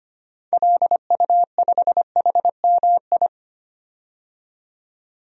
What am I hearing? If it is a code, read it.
Morse code, heard as a keyed tone, about 25 wpm, 700 Hz. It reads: LU5HMI
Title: DXCC-100 Call Signs - 6 Characters - 25wpm